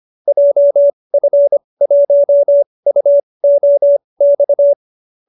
Morse code, heard as a keyed tone, about 25 wpm, spaced as for 20 wpm, 590 Hz. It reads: JF1UOX
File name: qso216 20